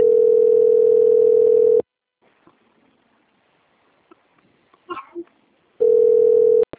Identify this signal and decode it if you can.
Ringback tone (North America), 440 + 480 Hz.